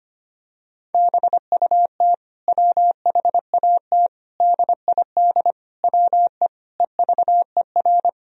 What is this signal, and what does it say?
Morse code, heard as a keyed tone, about 25 wpm, 700 Hz. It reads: BUT WHAT DID WE EVER
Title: Sentences from Top 400 Words and Abbreviations 25wpm